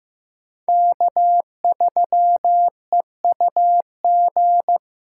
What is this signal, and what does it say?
Morse code, heard as a keyed tone, about 15 wpm, 700 Hz. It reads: K3EUG
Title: USA Call Signs - 5 Characters - 4x Character Spacing - 15wpm